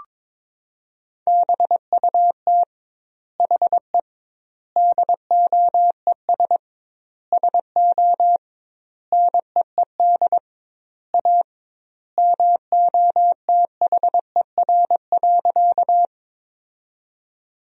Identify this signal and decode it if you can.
Morse code, heard as a keyed tone, about 22 wpm, 700 Hz. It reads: BUT HE DOES SO NEED A MOTHER.
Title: 2x Word-Spacing - Sentences from Top 200 Words - 22wpm